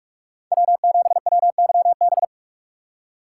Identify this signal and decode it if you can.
Morse code, heard as a keyed tone, about 46 wpm, 700 Hz. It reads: W7WYB
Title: USA Call Signs - 5 Characters - 4x Character Spacing - 45wpm